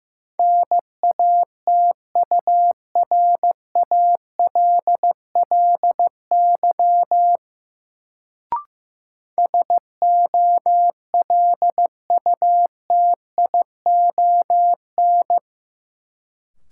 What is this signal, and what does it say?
Morse code, heard as a keyed tone, about 15 wpm, 700 Hz. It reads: NATURALLY SOLUTION
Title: Sentences from Top 1300 Words - New Word Drill - 15wpm